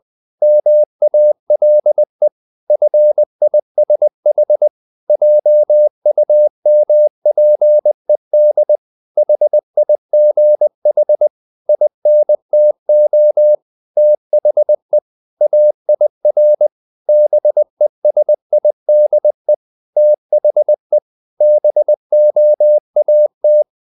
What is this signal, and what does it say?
Morse code, heard as a keyed tone, about 20 wpm, 600 Hz. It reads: MALE FISH JUMPED HIGH INTO THE AIR BESIDE THE BOAT
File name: the old man and the sea / the old man and the sea part 1 20wpm